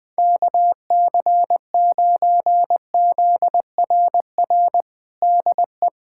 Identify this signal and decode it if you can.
Morse code, heard as a keyed tone, about 21 wpm, 700 Hz. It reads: KC9ZRR DE
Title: USA Callsign DE Callsign 20wpm